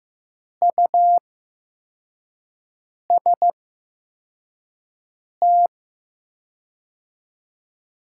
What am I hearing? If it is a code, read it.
Morse code, heard as a keyed tone, about 15 wpm, spaced as for 4 wpm, 700 Hz. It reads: UST
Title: 8x Character Spacing - Top 200 Words - 15wpm